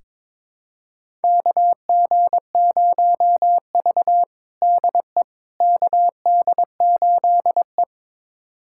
Morse code, heard as a keyed tone, about 22 wpm, 700 Hz. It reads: KG0V DE KD8E
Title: USA Callsign DE Callsign - 4 Characters - 22wpm